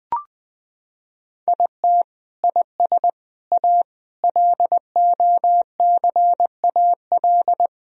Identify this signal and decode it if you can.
Morse code, heard as a keyed tone, about 20 wpm, 700 Hz. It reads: IT IS A LOCAL
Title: Sentences from Top 1000 Words - Review Part 3 - 20wpm